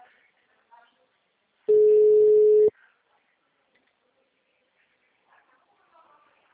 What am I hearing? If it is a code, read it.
Ringback tone (Europe, 425 Hz), with speech.
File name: out- -2015- - - .252